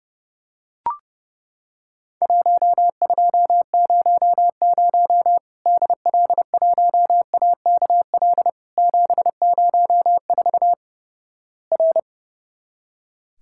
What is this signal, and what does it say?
Morse code, heard as a keyed tone, about 31 wpm, 700 Hz. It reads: 1200 DL1AKL 704 R
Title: Worked All Europe DX Contest - QTC Exchange - 30wpm